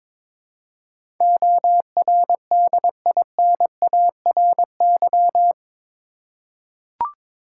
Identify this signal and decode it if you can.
Morse code, heard as a keyed tone, about 22 wpm, 700 Hz. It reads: ORDINARY
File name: practice / 22wpm / Eight Letter Words - 22wpm